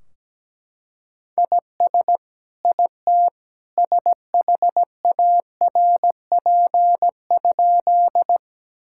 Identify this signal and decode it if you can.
Morse code, heard as a keyed tone, about 17 wpm, 700 Hz. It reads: IS IT SHARP?
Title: Sentences from Top 1600 Words 17wpm